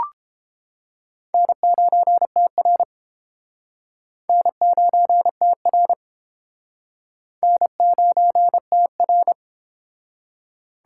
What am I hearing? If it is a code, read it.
Morse code, heard as a keyed tone, about 32 wpm, 700 Hz. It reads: N9TR N9TR N9TR